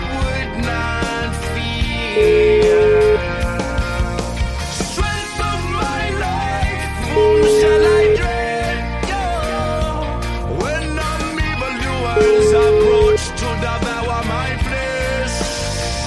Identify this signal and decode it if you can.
Ringback tone (Europe, 425 Hz), with noise.